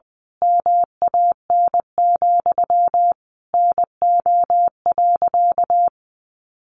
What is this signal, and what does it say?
Morse code, heard as a keyed tone, about 20 wpm, 700 Hz. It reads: MAN, NO.
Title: A Sound Of Thunder 20wpm